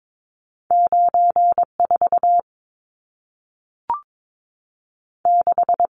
Morse code, heard as a keyed tone, about 22 wpm, 700 Hz. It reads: 94 6
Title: Numbers - 2-Digit 22wpm